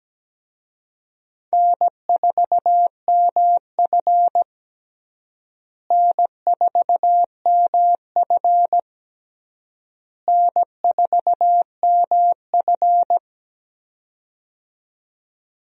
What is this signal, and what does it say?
Morse code, heard as a keyed tone, about 17 wpm, 700 Hz. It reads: N4MF N4MF N4MF
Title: USA Call Signs - 4 Characters - Repeated 3x - 17wpm